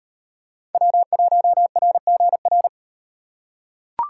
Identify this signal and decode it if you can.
Morse code, heard as a keyed tone, about 37 wpm, 700 Hz. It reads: W1RGR